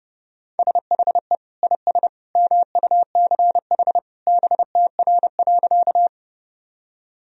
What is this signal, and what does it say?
Morse code, heard as a keyed tone, about 32 wpm, 700 Hz. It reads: SHE IS MUCH BTR.
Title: Sentences from Top 400 Words and Abbreviations 30wpm